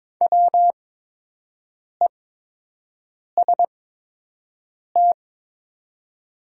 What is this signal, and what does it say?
Morse code, heard as a keyed tone, about 22 wpm, spaced as for 6 wpm, 700 Hz. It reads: WEST
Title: 8x Character Spacing - Top 500 Words - 22wpm